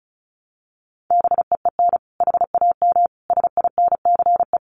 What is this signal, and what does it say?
Morse code, heard as a keyed tone, about 34 wpm, 700 Hz. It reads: BEEN HAM SINCE
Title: Common QSO Phrases 35wpm